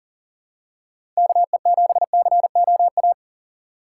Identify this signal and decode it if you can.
Morse code, heard as a keyed tone, about 40 wpm, 700 Hz. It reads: KE7COA
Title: USA Call Signs - 6 Characters - 2x Character Spacing - 40wpm